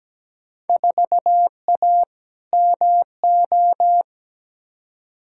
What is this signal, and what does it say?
Morse code, heard as a keyed tone, about 17 wpm, 700 Hz. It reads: 4A MO
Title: Field Day Exchanges - 17wpm